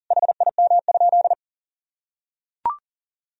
Morse code, heard as a keyed tone, about 40 wpm, 700 Hz. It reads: HIM?
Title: Sentences from Top 100 Words and Abbreviations 40wpm